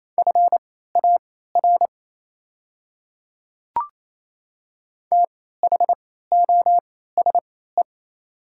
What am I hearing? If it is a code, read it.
Morse code, heard as a keyed tone, about 28 wpm, spaced as for 16 wpm, 700 Hz. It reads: FAR THOSE
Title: 3x Character Spacing - Top 300 Words - 28wpm